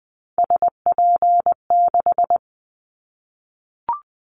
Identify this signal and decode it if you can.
Morse code, heard as a keyed tone, about 20 wpm, 700 Hz. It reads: SP6
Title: Callsign Prefixes for Top 25 DXCC - 20wpm